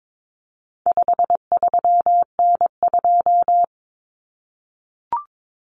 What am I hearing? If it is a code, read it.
Morse code, heard as a keyed tone, about 22 wpm, 700 Hz. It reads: 53N2
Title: Common Cut Numbers - 4-Digit 22wpm